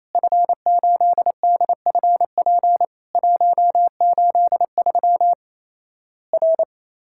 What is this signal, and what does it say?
Morse code, heard as a keyed tone, about 27 wpm, 700 Hz. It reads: F8DFP 183 R
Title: Worked All Europe DX Contest - QTC Exchange - 28wpm